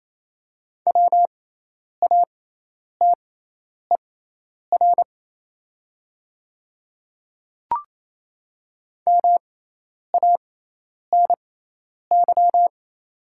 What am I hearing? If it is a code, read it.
Morse code, heard as a keyed tone, about 27 wpm, spaced as for 10 wpm, 690 Hz. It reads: WATER MANY
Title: 6x Character Spacing - Top 100 Words - 28wpm